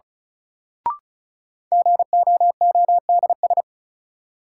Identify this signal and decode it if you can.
Morse code, heard as a keyed tone, about 36 wpm, 700 Hz. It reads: GOODS
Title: Sentences from Top 1000 Words - Review Part 6 - New Word Drill - 35wpm